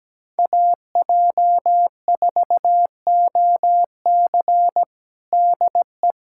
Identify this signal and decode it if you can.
Morse code, heard as a keyed tone, about 17 wpm, 700 Hz. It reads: AJ4OC DE
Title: Practice QSO - Exchange 4 - Ending - 17wpm